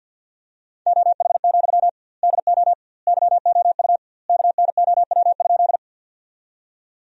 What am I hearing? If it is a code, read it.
Morse code, heard as a keyed tone, about 53 wpm, 700 Hz. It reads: OH, DO YOU KNOW?
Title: 2x Word-Spacing - Sentences from Top 500 Words - 50wpm